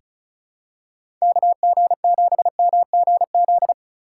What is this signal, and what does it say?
Morse code, heard as a keyed tone, about 36 wpm, 700 Hz. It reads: KG7MGZ
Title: USA Call Signs - 6 Characters - 4x Character Spacing - 35wpm